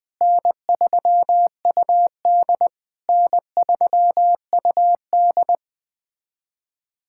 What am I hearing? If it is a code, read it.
Morse code, heard as a keyed tone, about 21 wpm, 700 Hz. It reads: N3UD N3UD
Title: POTA QSO Practice - 20wpm